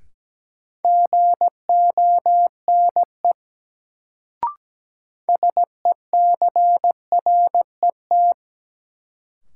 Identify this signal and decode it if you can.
Morse code, heard as a keyed tone, about 17 wpm, 700 Hz. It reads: GONE SECRET
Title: Sentences from Top 1200 Words - New Word Drill Alt - 17wpm